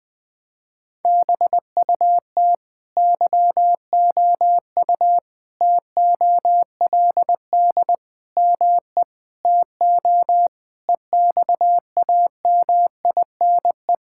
Morse code, heard as a keyed tone, about 20 wpm, 700 Hz. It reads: BUT YOU TOLD ME TO EXAMINE
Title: Sentences from Top 1600 Words 20wpm